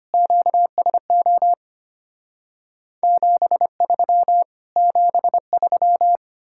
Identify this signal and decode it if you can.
Morse code, heard as a keyed tone, about 33 wpm, 700 Hz. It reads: QSO 73 73